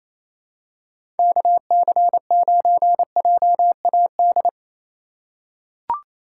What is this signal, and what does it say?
Morse code, heard as a keyed tone, about 28 wpm, 700 Hz. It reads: KC9JAD
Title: USA Call Signs - 6 Characters - 3x Character Spacing - 28wpm